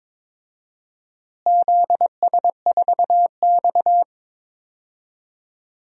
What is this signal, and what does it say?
Morse code, heard as a keyed tone, about 22 wpm, 700 Hz. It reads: ZS4X
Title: DXCC-100 Call Signs - 4 Characters - Repeated 3x - 22wpm